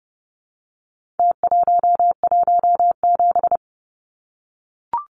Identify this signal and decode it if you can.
Morse code, heard as a keyed tone, about 30 wpm, 700 Hz. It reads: T117
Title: Numbers - Contest Style - 4-Digit 30wpm